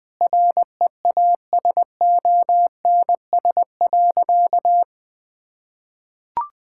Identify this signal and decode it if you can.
Morse code, heard as a keyed tone, about 21 wpm, 700 Hz. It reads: REASONS.
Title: Sentences with 4-Word Phrases - 20wpm